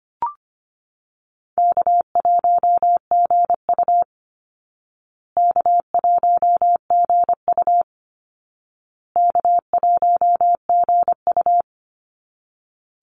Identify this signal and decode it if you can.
Morse code, heard as a keyed tone, about 25 wpm, 700 Hz. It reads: K1GU K1GU K1GU
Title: Easy Call Signs Repeated 3x - 25wpm